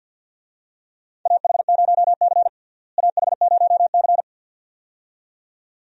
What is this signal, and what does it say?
Morse code, heard as a keyed tone, about 46 wpm, 700 Hz. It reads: AH0C AH0C
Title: USA Call Signs - 4 Characters - Repeated 2x - 50wpm